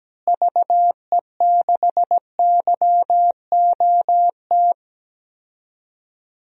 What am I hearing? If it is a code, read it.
Morse code, heard as a keyed tone, about 17 wpm, 700 Hz. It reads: VE6YOT
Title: DXCC-100 Call Signs - 6 Characters - Repeated 3x - 17wpm